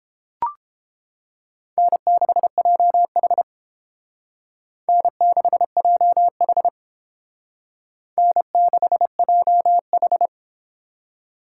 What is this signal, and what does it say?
Morse code, heard as a keyed tone, about 29 wpm, 700 Hz. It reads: N6JH N6JH N6JH